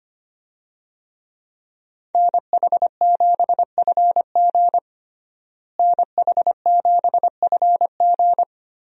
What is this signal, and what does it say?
Morse code, heard as a keyed tone, about 25 wpm, 700 Hz. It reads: NH7FG NH7FG
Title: USA Call Signs - 5 Characters - Repeated 2x - 25wpm